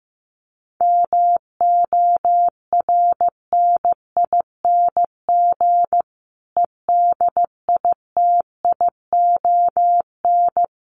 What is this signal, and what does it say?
Morse code, heard as a keyed tone, about 15 wpm, 700 Hz. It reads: MORNING EDITION
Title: Sets of 5 Words Encore 15wpm